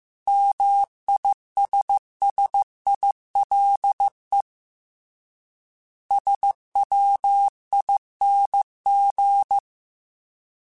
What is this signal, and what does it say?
Morse code, heard as a keyed tone, about 15 wpm, 800 Hz. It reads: MISSILE SWING
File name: Short Runs / Short Runs-mp3s / Words @13